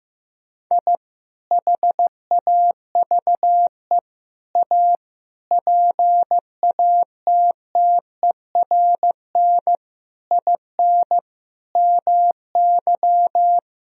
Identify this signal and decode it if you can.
Morse code, heard as a keyed tone, about 15 wpm, 700 Hz. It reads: I HAVE A PATTERN IN MY
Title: Sentences from Top 1000 Words - Review Part 2 - 15wpm